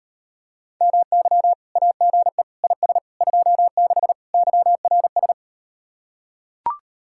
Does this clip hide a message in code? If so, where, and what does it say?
Yes: Morse code from 0.8 to 5.3 s, heard as a keyed tone, about 39 wpm, 700 Hz. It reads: MY AGE IS 26 YRS